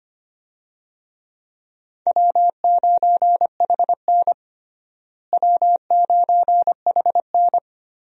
Morse code, heard as a keyed tone, about 25 wpm, 700 Hz. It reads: W9HN W9HN
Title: USA Call Signs - 4 Characters - Repeated 3x - 25wpm